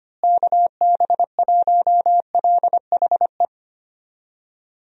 Morse code, heard as a keyed tone, about 25 wpm, 700 Hz. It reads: KB1LHE
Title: USA Call Signs - 6 Characters - Repeated 2x - 25wpm